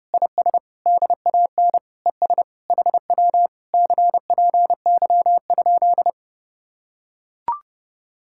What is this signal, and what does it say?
Morse code, heard as a keyed tone, about 32 wpm, 700 Hz. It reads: IS DAN ES HW CPY?